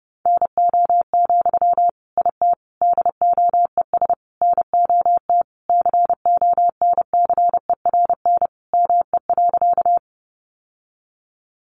Morse code, heard as a keyed tone, about 30 wpm, 700 Hz. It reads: NO, IT DOES NOT CONCERN ME.
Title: Sentences from Top 1000 Words 30wpm